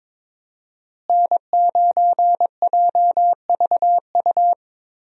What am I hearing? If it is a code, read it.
Morse code, heard as a keyed tone, about 22 wpm, 700 Hz. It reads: N9JVU
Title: USA Call Signs - 5 Characters - Repeated 2x - 22wpm